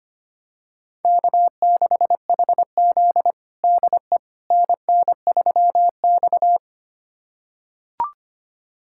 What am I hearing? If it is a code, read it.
Morse code, heard as a keyed tone, about 25 wpm, 700 Hz. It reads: K6HZ DE NN3X
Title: USA Callsign DE Callsign - 4 Characters - 25wpm